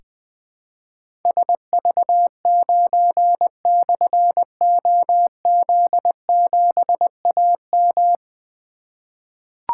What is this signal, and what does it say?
Morse code, heard as a keyed tone, about 20 wpm, 700 Hz. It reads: SV9/OZ7AM
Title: Difficult Call Signs 20wpm